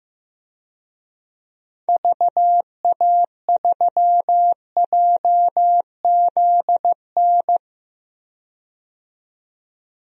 Morse code, heard as a keyed tone, about 15 wpm, 700 Hz. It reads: VA3JZN
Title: DXCC-100 Call Signs - 6 Characters - 15wpm